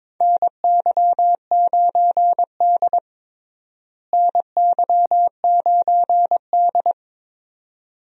Morse code, heard as a keyed tone, about 22 wpm, 700 Hz. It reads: NY9D NY9D
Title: USA Call Signs - 4 Characters - Repeated 2x - 22wpm